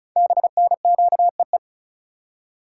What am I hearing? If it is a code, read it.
Morse code, heard as a keyed tone, about 36 wpm, 690 Hz. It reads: BNQEE
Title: Code Groups - 5 Letters - 35wpm